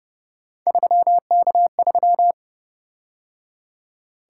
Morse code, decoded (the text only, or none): 3K3